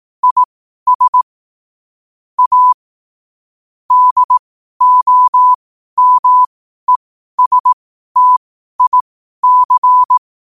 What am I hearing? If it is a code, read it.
Morse code, heard as a keyed tone, about 18 wpm, spaced as for 12 wpm, 1000 Hz. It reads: IS A DOMESTIC
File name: qotdfiles / 12wpm